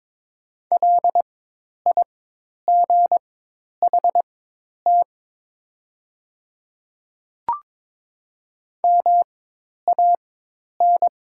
Morse code, heard as a keyed tone, about 22 wpm, spaced as for 10 wpm, 700 Hz. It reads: LIGHT MAN
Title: 4x Character Spacing - Top 200 Words - 22wpm